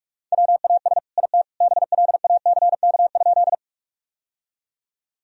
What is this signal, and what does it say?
Morse code, heard as a keyed tone, about 46 wpm, 700 Hz. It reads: WAS IT BLACK?
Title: Sentences from Top 1000 Words - Review Part 2 - 45wpm